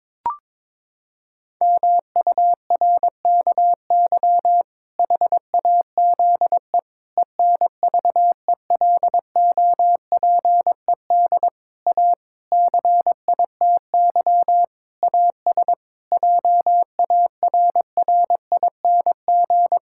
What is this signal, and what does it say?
Morse code, heard as a keyed tone, about 22 wpm, 700 Hz. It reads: MURKY HAZE ENVELOPED A CITY AS JARRING
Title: Pangram Challenge v2 - 22wpm